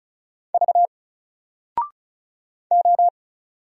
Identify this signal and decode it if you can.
Morse code, heard as a keyed tone, about 33 wpm, 700 Hz. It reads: V O
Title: Single Letter-Number 35wpm